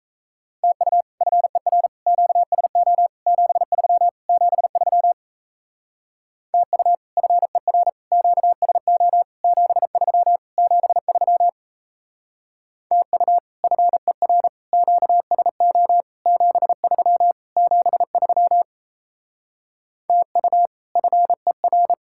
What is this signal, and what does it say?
Morse code, heard as a keyed tone, about 35 wpm, 700 Hz. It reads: TU FER QSO 73 73 TU FER QSO 73 73 TU FER QSO 73 73 TU FER